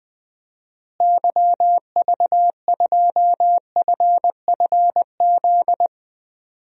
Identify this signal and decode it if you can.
Morse code, heard as a keyed tone, about 21 wpm, 700 Hz. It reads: YV2FFZ
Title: DXCC-100 Call Signs - 6 Characters - Repeated 3x - 20wpm